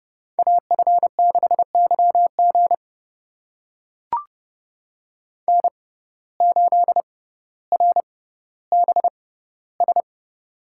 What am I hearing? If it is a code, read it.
Morse code, heard as a keyed tone, about 30 wpm, spaced as for 15 wpm, 700 Hz. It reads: AF6YG N8RBS
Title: USA Call Signs - 5 Characters - 6x Character Spacing - 30wpm